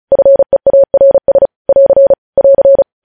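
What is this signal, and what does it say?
Morse code, heard as a keyed tone, about 36 wpm, 550 Hz. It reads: FEARS <AR> <AR>
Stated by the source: Morse practice sending news headlines at 35 wpm.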